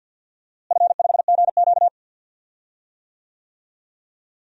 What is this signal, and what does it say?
Morse code, heard as a keyed tone, about 52 wpm, 700 Hz. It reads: F5GQ
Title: DXCC-100 Call Signs - 4 Characters - Repeated 3x - 50wpm